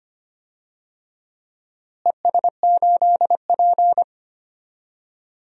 Morse code, heard as a keyed tone, about 25 wpm, 700 Hz. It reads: ES8P
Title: DXCC-100 Call Signs - 4 Characters - Repeated 2x - 25wpm